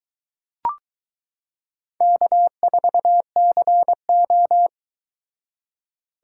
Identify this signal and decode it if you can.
Morse code, heard as a keyed tone, about 22 wpm, 700 Hz. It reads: K4CO